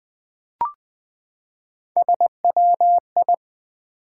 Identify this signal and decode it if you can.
Morse code, heard as a keyed tone, about 20 wpm, 700 Hz. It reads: SWI
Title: Build A Word - 3 Letters At A Time - 8-Letter Words - 20wpm